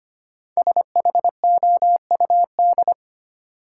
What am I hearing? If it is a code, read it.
Morse code, heard as a keyed tone, about 25 wpm, 700 Hz. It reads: SHOUD
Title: Sentences from Top 1000 Words - Review Part 4 - New Word Drill - 25wpm